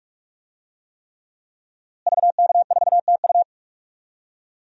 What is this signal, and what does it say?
Morse code, heard as a keyed tone, about 41 wpm, 700 Hz. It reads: VK4TU
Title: DXCC-100 Call Signs - 5 Characters - Repeated 3x - 45wpm